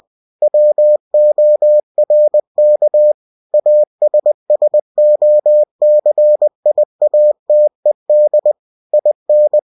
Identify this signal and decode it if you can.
Morse code, heard as a keyed tone, about 20 wpm, 600 Hz. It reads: WORK ASSOCIATED IN
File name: cw-books / winnie-the-pooh / winnie-20wpm-page-0113